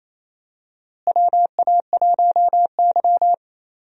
Morse code, heard as a keyed tone, about 28 wpm, 700 Hz. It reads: WA1Y
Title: USA Call Signs - 4 Characters - Repeated 2x - 28wpm